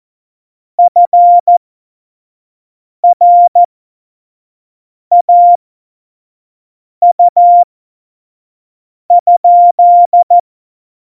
Morse code, heard as a keyed tone, about 14 wpm, spaced as for 5 wpm, 700 Hz. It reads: FRAU?
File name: Morsekurs / Witze / Witz 13. Oktober 2020, 25 F70